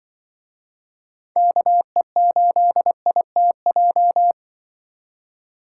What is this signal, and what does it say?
Morse code, heard as a keyed tone, about 24 wpm, 700 Hz. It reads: KE8ITJ
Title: Speed Racing - USA 2x3 Call Signs 20wpm